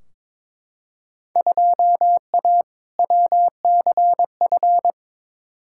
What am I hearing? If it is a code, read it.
Morse code, heard as a keyed tone, about 22 wpm, 700 Hz. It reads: 2A WCF
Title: Field Day Exchanges - 22wpm